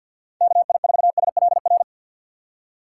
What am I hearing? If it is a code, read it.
Morse code, heard as a keyed tone, about 49 wpm, 700 Hz. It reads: KI4SLR
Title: USA Call Signs - 6 Characters - 3x Character Spacing - 50wpm